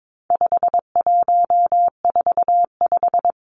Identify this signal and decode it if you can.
Morse code, heard as a keyed tone, about 22 wpm, 700 Hz. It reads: 5145
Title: Numbers - 4-Digit 22wpm